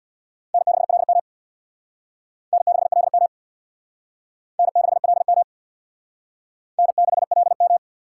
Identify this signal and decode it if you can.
Morse code, heard as a keyed tone, about 61 wpm, 700 Hz. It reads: N6LM N6LM N6LM N6LM
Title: Speed Racing - USA 1x2 Call Signs 50wpm